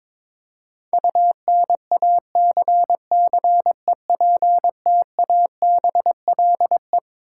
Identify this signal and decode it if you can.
Morse code, heard as a keyed tone, about 22 wpm, 700 Hz. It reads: UNACCEPTABLE
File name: Twelve Letter Words - 22wpm